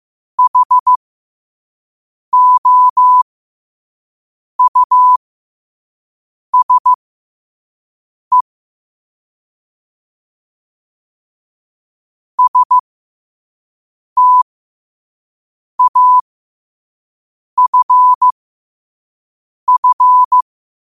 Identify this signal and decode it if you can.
Morse code, heard as a keyed tone, about 15 wpm, spaced as for 5 wpm, 1000 Hz. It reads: HOUSE STAFF
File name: qotdfiles / 5wpm